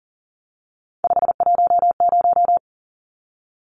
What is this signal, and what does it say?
Morse code, heard as a keyed tone, about 40 wpm, 700 Hz. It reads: 510
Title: Numbers - 3-Digit 40wpm